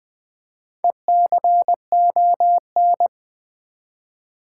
Morse code, heard as a keyed tone, about 20 wpm, 700 Hz. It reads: ECON
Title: Build A Word - 2 Letters At A Time - 8-Letter Words - 20wpm